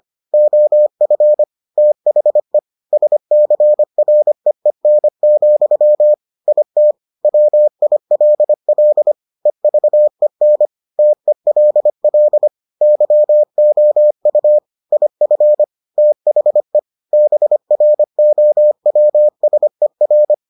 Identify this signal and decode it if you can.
Morse code, heard as a keyed tone, about 25 wpm, 600 Hz. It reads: OF THE SCREEN, IT WILL EVEN TELL YOU IF THE BROWSER